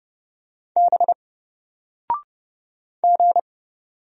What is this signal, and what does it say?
Morse code, heard as a keyed tone, about 31 wpm, 700 Hz. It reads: B G
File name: Single-Letter-Number-30wpm